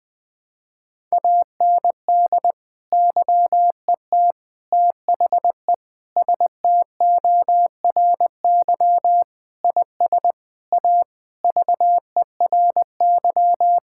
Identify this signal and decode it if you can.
Morse code, heard as a keyed tone, about 20 wpm, 700 Hz. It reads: AND YET THE STORY IS A VERY
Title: Sentences from Top 1000 Words - Review Part 6 - 20wpm